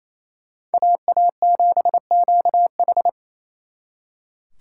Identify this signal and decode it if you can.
Morse code, heard as a keyed tone, about 37 wpm, 700 Hz. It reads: AA7QH